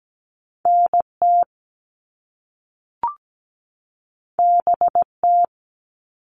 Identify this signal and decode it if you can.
Morse code, heard as a keyed tone, about 17 wpm, 700 Hz. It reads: NT BT
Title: All Cut Numbers - 2-Digit 17wpm